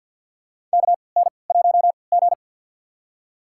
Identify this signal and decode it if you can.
Morse code, heard as a keyed tone, about 47 wpm, spaced as for 27 wpm, 700 Hz. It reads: KN1G
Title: USA Call Signs - 3x Character Spacing - 50wpm